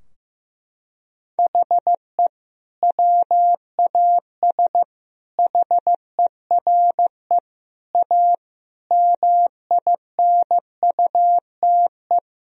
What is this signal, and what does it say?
Morse code, heard as a keyed tone, about 16 wpm, 700 Hz. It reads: HE WAS HERE A MINUTE
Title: Sentences from Top 1000 Words - Review Part 3 - 15wpm